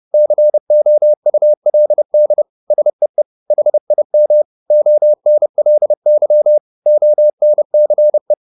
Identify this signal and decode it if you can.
Morse code, heard as a keyed tone, about 30 wpm, 600 Hz. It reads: COULD SEE HIM ONLY ONCE
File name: the old man and the sea / the old man and the sea 1 30wpm